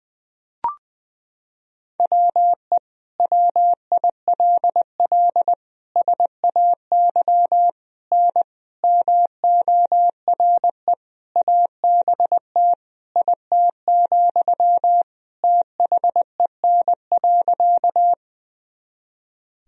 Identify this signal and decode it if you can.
Morse code, heard as a keyed tone, about 20 wpm, 700 Hz. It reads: WE WILL SAY N MORE ABT IT, THEN.
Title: Sentences from Top 200 Words and Abbreviations 20wpm